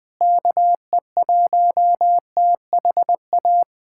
Morse code, heard as a keyed tone, about 21 wpm, 700 Hz. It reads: KE1THA
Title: USA Callsign DE Callsign 20wpm